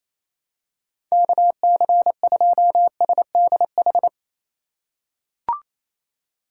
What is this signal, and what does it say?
Morse code, heard as a keyed tone, about 28 wpm, 700 Hz. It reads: KC2SDH
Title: USA Call Signs - 6 Characters - 2x Character Spacing - 28wpm